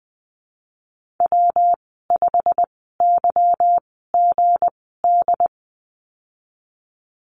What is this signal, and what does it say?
Morse code, heard as a keyed tone, about 20 wpm, spaced as for 15 wpm, 700 Hz. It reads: W5YGD
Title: 179 Lesson 29g - Slash - Call Signs 20wpm